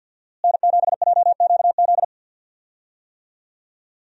Morse code, heard as a keyed tone, about 52 wpm, 700 Hz. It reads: N7JQZ
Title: USA Call Signs - 5 Characters - Repeated 3x - 50wpm